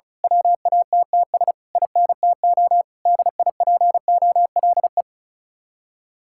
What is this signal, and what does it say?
Morse code, heard as a keyed tone, about 33 wpm, 700 Hz. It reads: WATTS INTO DIPOLE
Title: Speed-Racing Common QSO Phrases Encore - 35wpm